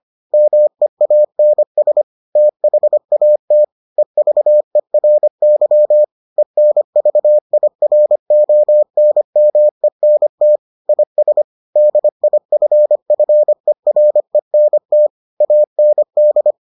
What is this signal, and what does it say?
Morse code, heard as a keyed tone, about 25 wpm, 600 Hz. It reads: MEANS THAT EVERY ENVIRONMENT IS DIFFERENT AND